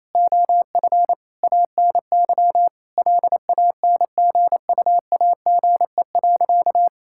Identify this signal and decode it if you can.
Morse code, heard as a keyed tone, about 28 wpm, 700 Hz. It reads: OF ANY LANGUAGE.
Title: Sentences from Top 1600 Words 28wpm